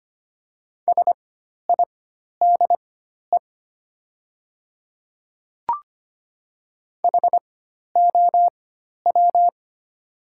Morse code, heard as a keyed tone, about 25 wpm, spaced as for 12 wpm, 700 Hz. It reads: SIDE HOW
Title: 4x Character Spacing - Top 100 Words - 25wpm